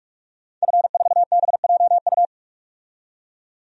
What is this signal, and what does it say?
Morse code, heard as a keyed tone, about 43 wpm, 700 Hz. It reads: F4BJU
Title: DXCC-100 Call Signs - 5 Characters - 45wpm